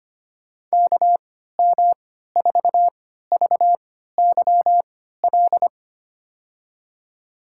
Morse code, heard as a keyed tone, about 25 wpm, spaced as for 14 wpm, 700 Hz. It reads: KM4VYL
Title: USA Call Signs - 6 Characters - 3x Character Spacing - 25wpm